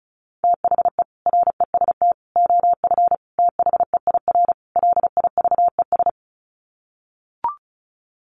Morse code, heard as a keyed tone, about 34 wpm, 700 Hz. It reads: THE REST OF THEIR LIVES
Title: Sets of 5 Words Encore 35wpm